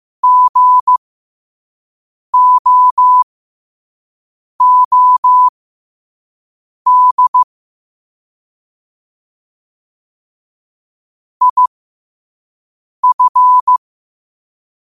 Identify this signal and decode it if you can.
Morse code, heard as a keyed tone, about 15 wpm, spaced as for 5 wpm, 1000 Hz. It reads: GOOD IF
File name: qotdfiles / 5wpm